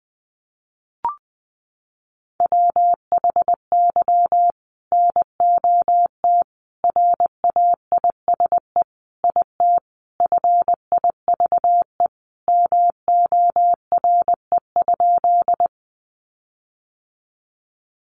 Morse code, heard as a keyed tone, about 19 wpm, 700 Hz. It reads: WHY NOT RAISE IT FIVE MORE?
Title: Sentences from Top 1400 Words 20wpm